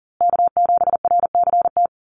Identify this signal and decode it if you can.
Morse code, heard as a keyed tone, about 40 wpm, 700 Hz. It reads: K7RCT
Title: USA Call Signs Repeated 3x 40wpm